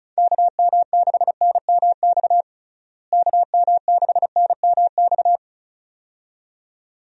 Morse code, heard as a keyed tone, about 36 wpm, 700 Hz. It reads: KM6NMX KM6NMX
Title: USA Call Signs - 6 Characters - Repeated 2x - 35wpm